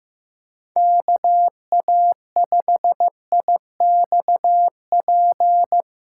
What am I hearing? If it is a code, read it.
Morse code, heard as a keyed tone, about 18 wpm, 700 Hz. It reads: KA5IXP
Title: Speed Racing - USA 2x3 Call Signs 15wpm